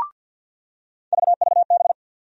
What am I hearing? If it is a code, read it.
Morse code, heard as a keyed tone, about 47 wpm, 700 Hz. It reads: VVB
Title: USA Call Sign Postfixes - 3 Letters 50wpm